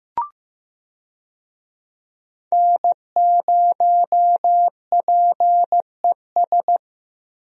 Morse code, heard as a keyed tone, about 15 wpm, 700 Hz. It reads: N0PES
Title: USA Call Signs - 5 Characters - Repeated 3x - 15wpm